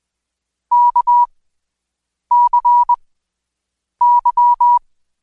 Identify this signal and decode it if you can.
Morse code, heard as a keyed tone, about 21 wpm, spaced as for 7 wpm, 970 Hz. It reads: KCY
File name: Code Quick / AudioDownld2 / Seg113-Day18-DayDrill 18